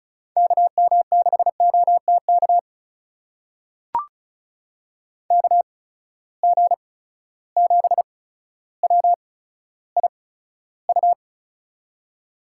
Morse code, heard as a keyed tone, about 36 wpm, spaced as for 15 wpm, 700 Hz. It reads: KM6OTK KG7WIU
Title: USA Call Signs - 6 Characters - 8x Character Spacing - 35wpm